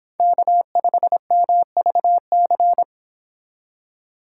Morse code, heard as a keyed tone, about 26 wpm, 700 Hz. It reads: K5MVC